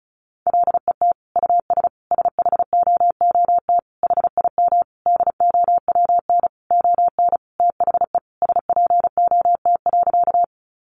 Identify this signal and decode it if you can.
Morse code, heard as a keyed tone, about 34 wpm, 700 Hz. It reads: LET US SHOOT HIM DOWN ON THE SPOT.
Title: Sentences from Top 1200 Words 35wpm